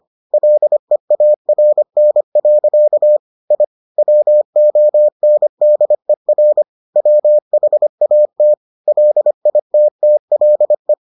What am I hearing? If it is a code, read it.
Morse code, heard as a keyed tone, about 25 wpm, 600 Hz. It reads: LEARN. I WONDER WHAT LITTLE